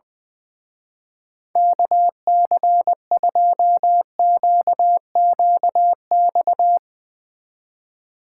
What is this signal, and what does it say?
Morse code, heard as a keyed tone, about 20 wpm, 700 Hz. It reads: KC2QQX
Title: Speed Racing - USA 2x3 Call Signs 20wpm